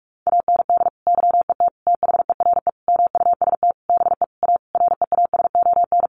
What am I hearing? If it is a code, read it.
Morse code, heard as a keyed tone, about 44 wpm, 700 Hz. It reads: AND YET THERE MUST BE A REASON
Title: Sentences from Top 1300 Words 45wpm